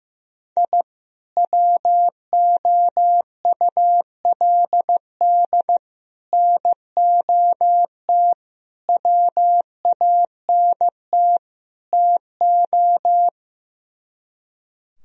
Morse code, heard as a keyed tone, about 15 wpm, 700 Hz. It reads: I WOULD NOT WANT TO
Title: Speed-Racing - Sets of 5 Words 15wpm